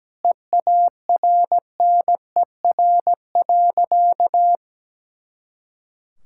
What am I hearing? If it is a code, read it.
Morse code, heard as a keyed tone, about 17 wpm, 700 Hz. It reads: EARNER.
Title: Pangram Challenge v2 - 17wpm